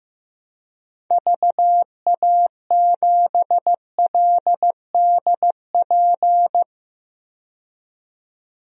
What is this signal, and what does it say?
Morse code, heard as a keyed tone, about 16 wpm, 700 Hz. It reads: VA7LDP
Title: DXCC-100 Call Signs - 6 Characters - 15wpm